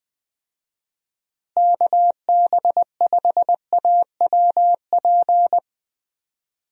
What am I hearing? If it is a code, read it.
Morse code, heard as a keyed tone, about 20 wpm, 700 Hz. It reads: KB5AWP
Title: USA Call Signs - 6 Characters - Repeated 2x - 20wpm